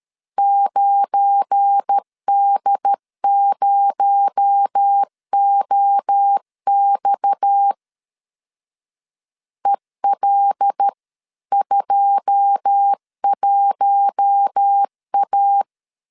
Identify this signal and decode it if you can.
Morse code, heard as a keyed tone, about 13 wpm, 800 Hz. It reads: 9D0OX EL21A
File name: CW / DET / 1 9 1500C S B 12wpm 3819